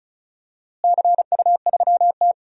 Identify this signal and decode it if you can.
Morse code, heard as a keyed tone, about 34 wpm, 700 Hz. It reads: CU3T
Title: DXCC-100 Call Signs - 4 Characters - Repeated 2x - 35wpm